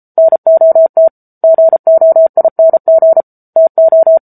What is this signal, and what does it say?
Morse code, heard as a keyed tone, about 34 wpm, 650 Hz. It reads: NOT GOING TO